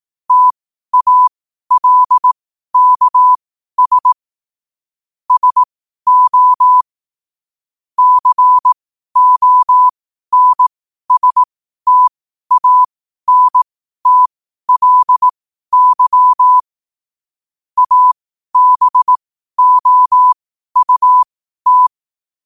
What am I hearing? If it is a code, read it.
Morse code, heard as a keyed tone, about 18 wpm, spaced as for 12 wpm, 1000 Hz. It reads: TALKS SO CONSTANTLY ABOUT
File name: qotdfiles / 12wpm